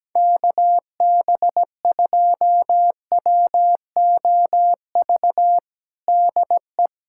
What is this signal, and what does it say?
Morse code, heard as a keyed tone, about 17 wpm, 700 Hz. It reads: KB2WOV DE
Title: USA Callsign DE Callsign - 6 Characters - 17wpm